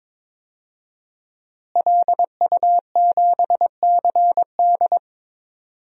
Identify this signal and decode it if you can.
Morse code, heard as a keyed tone, about 22 wpm, 700 Hz. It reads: LU7CD
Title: DXCC-100 Call Signs - 5 Characters - Repeated 3x - 22wpm